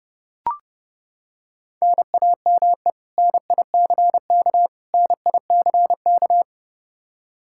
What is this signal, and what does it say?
Morse code, heard as a keyed tone, about 30 wpm, 700 Hz. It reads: NAME NICK NICK
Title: Practice QSO - Exchange 1 - Beginning - 30wpm